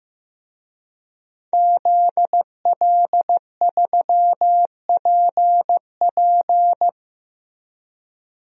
Morse code, heard as a keyed tone, about 15 wpm, 700 Hz. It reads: ZL3PP
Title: DXCC-100 Call Signs - 5 Characters - Repeated 2x - 15wpm